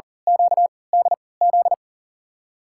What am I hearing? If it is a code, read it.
Morse code, heard as a keyed tone, about 40 wpm, spaced as for 23 wpm, 700 Hz. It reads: QDZ
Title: USA Call Signs - 3x Character Spacing - 40wpm